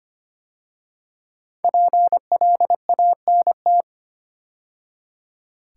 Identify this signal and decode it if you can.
Morse code, heard as a keyed tone, about 25 wpm, 700 Hz. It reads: PLANT
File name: 5 Letter Words - First 3 Letters Given - 25wpm